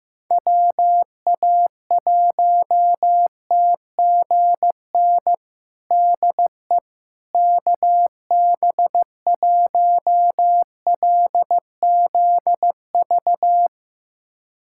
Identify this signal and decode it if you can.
Morse code, heard as a keyed tone, about 16 wpm, 700 Hz. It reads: WA1TGN DE KB1LZV
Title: USA Callsign DE Callsign - 6 Characters - 15wpm